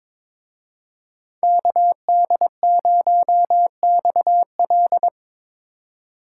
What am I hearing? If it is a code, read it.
Morse code, heard as a keyed tone, about 25 wpm, 700 Hz. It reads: KD0XL
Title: Speed Racing - USA 2x2 Call Signs 22wpm